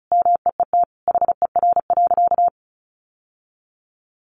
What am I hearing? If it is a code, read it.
Morse code, heard as a keyed tone, about 34 wpm, 700 Hz. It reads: MEET HER.
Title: Sentences from Top 1100 Words 35wpm